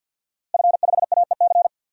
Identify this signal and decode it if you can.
Morse code, heard as a keyed tone, about 46 wpm, 700 Hz. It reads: F5AEC
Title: DXCC-100 Call Signs - 5 Characters - Repeated 3x - 50wpm